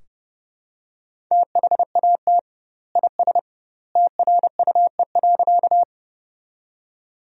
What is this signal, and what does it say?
Morse code, heard as a keyed tone, about 30 wpm, 700 Hz. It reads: THAT IS TRUE.
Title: 2x Word-Spacing - Sentences from Top 400 Words - 30wpm